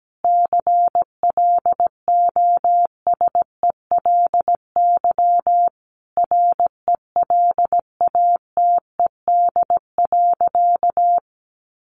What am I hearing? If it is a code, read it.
Morse code, heard as a keyed tone, about 17 wpm, 700 Hz. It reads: CLOSELY RELATED.
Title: Sentences from Top 1400 Words 17wpm